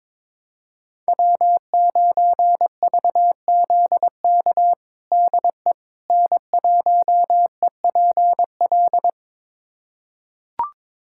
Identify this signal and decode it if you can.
Morse code, heard as a keyed tone, about 22 wpm, 700 Hz. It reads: W9VZK DE N1EPL
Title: USA Callsign DE Callsign - 5 Characters - 22wpm